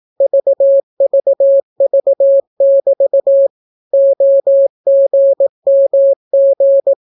Morse code, heard as a keyed tone, about 18 wpm, 550 Hz. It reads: VVV<BT> OGMG